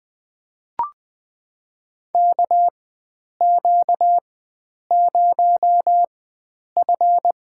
Morse code, heard as a keyed tone, about 20 wpm, spaced as for 9 wpm, 700 Hz. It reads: KQ0F
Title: USA Call Signs - 4 Characters - 4x Character Spacing - 20wpm